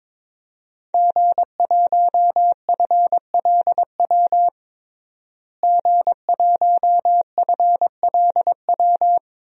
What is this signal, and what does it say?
Morse code, heard as a keyed tone, about 22 wpm, 700 Hz. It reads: G1FLW G1FLW
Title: DXCC-100 Call Signs - 5 Characters - Repeated 3x - 22wpm